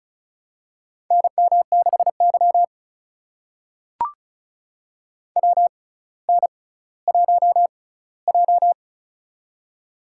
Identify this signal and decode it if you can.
Morse code, heard as a keyed tone, about 36 wpm, spaced as for 18 wpm, 700 Hz. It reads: NM6Y WN1J
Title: USA Call Signs - 4 Characters - 6x Character Spacing - 35wpm